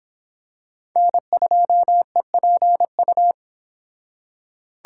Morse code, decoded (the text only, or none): N2EPU